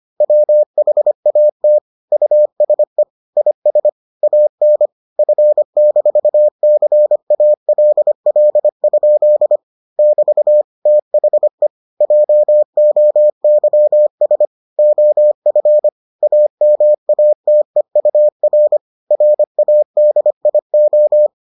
Morse code, heard as a keyed tone, about 25 wpm, 600 Hz. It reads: WHAT USE IS AN F-CALL? <BT> THE JOYS OF AMATEUR RADIO